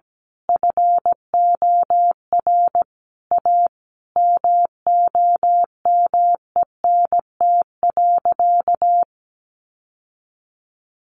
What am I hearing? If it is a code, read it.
Morse code, heard as a keyed tone, about 17 wpm, 700 Hz. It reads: FOR A MOMENT.
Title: Sentences from Top 1000 Words 17wpm